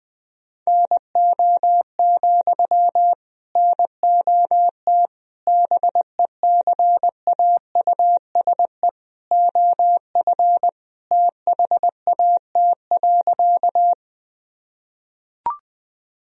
Morse code, heard as a keyed tone, about 20 wpm, 700 Hz. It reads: NO, NOT BECAUSE OF THAT.
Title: Sentences from Top 1000 Words - Review Part 3 - 20wpm